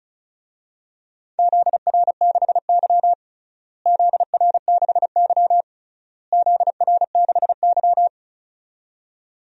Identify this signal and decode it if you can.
Morse code, heard as a keyed tone, about 36 wpm, 700 Hz. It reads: ZR6Y ZR6Y ZR6Y
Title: DXCC-100 Call Signs - 4 Characters - Repeated 3x - 35wpm